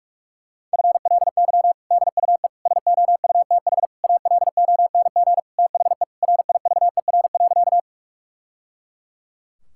Morse code, heard as a keyed tone, about 46 wpm, 700 Hz. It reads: FLY DUE SOUTH ALONG THE RIVER.
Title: Sentences from Top 1000 Words - Review Part 6 - 45wpm